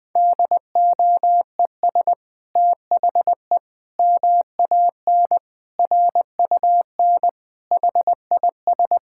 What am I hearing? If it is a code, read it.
Morse code, heard as a keyed tone, about 19 wpm, 700 Hz. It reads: DOES THE MAN RUN HIS
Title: Sentences from Top 1000 Words - Review Part 5 - 20wpm